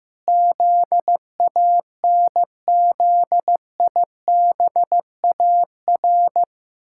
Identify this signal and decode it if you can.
Morse code, heard as a keyed tone, about 15 wpm, 700 Hz. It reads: ZANZIBAR
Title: Pangram Challenge v2 - 15wpm